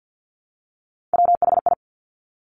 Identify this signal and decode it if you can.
Morse code, heard as a keyed tone, about 50 wpm, 700 Hz. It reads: WHI
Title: Top Three Letter Combinations in Words 50wpm